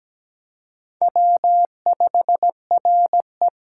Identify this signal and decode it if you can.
Morse code, heard as a keyed tone, about 20 wpm, 700 Hz. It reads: W5RE
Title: Speed Racing - USA 1x2 Call Signs 17wpm